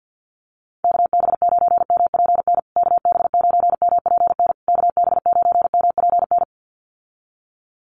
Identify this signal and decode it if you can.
Morse code, heard as a keyed tone, about 50 wpm, 700 Hz. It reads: KB9MPN KB9MPN KB9MPN
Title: USA Call Signs Repeated 3x 50wpm